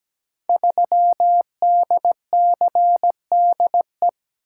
Morse code, heard as a keyed tone, about 17 wpm, 700 Hz. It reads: 3DCDE
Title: Code Groups - 5 Letter:Numbers - 17wpm